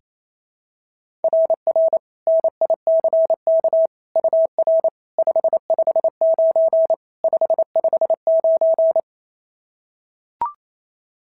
Morse code, heard as a keyed tone, about 28 wpm, 690 Hz. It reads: RR NICK UR 559 559
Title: SOTA QSO Practice 28wpm